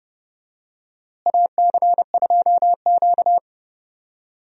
Morse code, heard as a keyed tone, about 30 wpm, 700 Hz. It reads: AC2Q
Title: USA Call Signs - 4 Characters - Repeated 2x - 30wpm